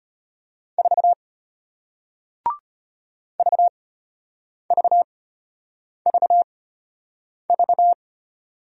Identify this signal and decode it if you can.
Morse code, heard as a keyed tone, about 34 wpm, spaced as for 8 wpm, 700 Hz. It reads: 4 VVVV